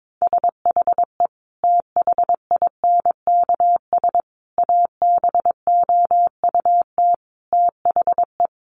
Morse code, heard as a keyed tone, about 22 wpm, 700 Hz. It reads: SHE THINKS ABOUT THE
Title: Easy Phrases 22wpm